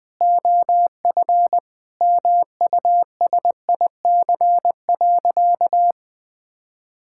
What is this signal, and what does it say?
Morse code, heard as a keyed tone, about 20 wpm, 700 Hz. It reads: OF MUSIC.
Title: Sentences with 3-Word Phrases - 20wpm